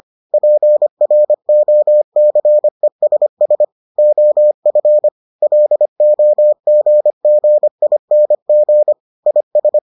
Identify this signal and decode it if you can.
Morse code, heard as a keyed tone, about 25 wpm, 600 Hz. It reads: PROCESS OF LOGGING IS